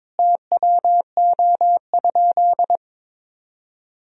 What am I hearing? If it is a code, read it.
Morse code, heard as a keyed tone, about 22 wpm, 700 Hz. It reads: TWO?
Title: Sentences from Top 1000 Words - Review Part 1 - 22wpm